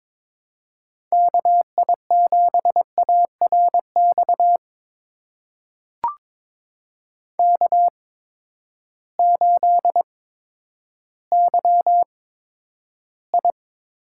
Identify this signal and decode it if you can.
Morse code, heard as a keyed tone, about 22 wpm, spaced as for 11 wpm, 700 Hz. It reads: KI7ARX K8YI
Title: USA Call Signs - 8x Character Spacing - 22wpm